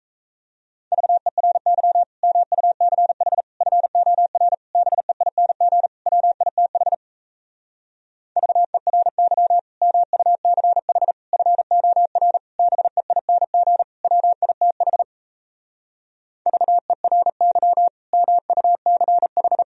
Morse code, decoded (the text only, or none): VERY MUCH FOR BEING WITH VERY MUCH FOR BEING WITH VERY MUCH